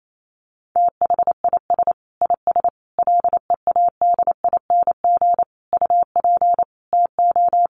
Morse code, heard as a keyed tone, about 28 wpm, 700 Hz. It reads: THIS IS LEADING UP TO
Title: Sentences from Top 1100 Words 28wpm